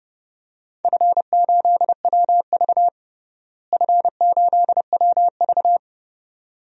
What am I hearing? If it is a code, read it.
Morse code, heard as a keyed tone, about 30 wpm, 700 Hz. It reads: F8WV F8WV
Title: DXCC-100 Call Signs - 4 Characters - Repeated 2x - 30wpm